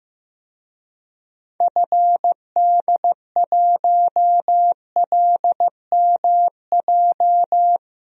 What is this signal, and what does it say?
Morse code, heard as a keyed tone, about 15 wpm, 700 Hz. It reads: FD1LMJ
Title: DXCC-100 Call Signs - 6 Characters - Repeated 2x - 15wpm